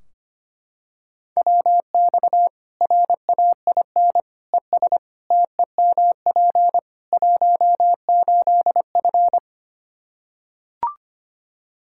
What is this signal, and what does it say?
Morse code, heard as a keyed tone, about 25 wpm, 700 Hz. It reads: WX RAIN ES TEMP 18F
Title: K1USN SST QSO Practice 25wpm